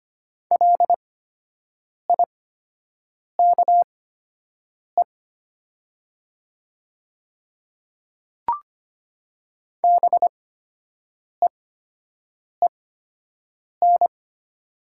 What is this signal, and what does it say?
Morse code, heard as a keyed tone, about 25 wpm, spaced as for 7 wpm, 700 Hz. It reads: LIKE BEEN
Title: 8x Character Spacing - Top 100 Words - 25wpm